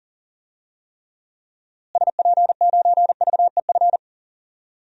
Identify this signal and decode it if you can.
Morse code, heard as a keyed tone, about 40 wpm, 700 Hz. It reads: SP9VEF
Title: DXCC-100 Call Signs - 6 Characters - Repeated 2x - 40wpm